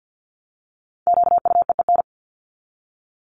Morse code, heard as a keyed tone, about 48 wpm, 700 Hz. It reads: QUEEN
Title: Top 1751-2000 Words 50wpm